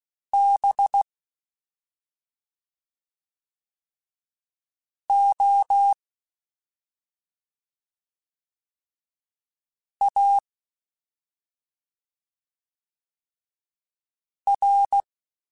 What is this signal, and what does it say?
Morse code, heard as a keyed tone, about 16 wpm, 800 Hz. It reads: B O A R
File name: Special INGs-mp3s / Special INGs-mp3s / Slow ing @10